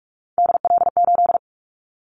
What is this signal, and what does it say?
Morse code, heard as a keyed tone, about 46 wpm, 700 Hz. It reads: DL8
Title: Callsign Prefixes for Top 50 DXCC - 45wpm